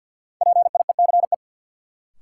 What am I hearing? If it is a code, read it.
Morse code, heard as a keyed tone, about 52 wpm, 700 Hz. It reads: PIECE
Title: Sentences from Top 400 Words - New Word Drill - 50wpm